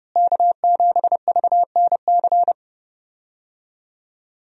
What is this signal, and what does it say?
Morse code, heard as a keyed tone, about 30 wpm, 700 Hz. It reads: K7VNC
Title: USA Call Signs - 5 Characters - Repeated 3x - 30wpm